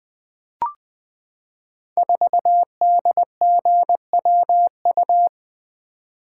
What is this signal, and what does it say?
Morse code, heard as a keyed tone, about 20 wpm, 700 Hz. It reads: 4DGWU
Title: Code Groups - 5 Letter:Numbers - 20wpm